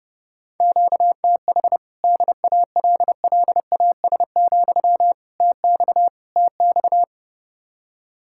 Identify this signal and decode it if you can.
Morse code, heard as a keyed tone, about 29 wpm, 700 Hz. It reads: QTH DALLAS, TX TX